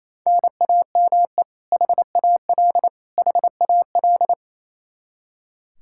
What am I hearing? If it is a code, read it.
Morse code, heard as a keyed tone, about 28 wpm, 700 Hz. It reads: NAME HAL HAL
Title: Practice QSO - Exchange 1 - Beginning - 28wpm